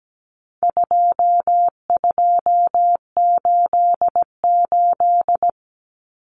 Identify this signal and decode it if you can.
Morse code, heard as a keyed tone, about 17 wpm, 700 Hz. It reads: 2288
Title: Common Cut Numbers - 4-Digit 17wpm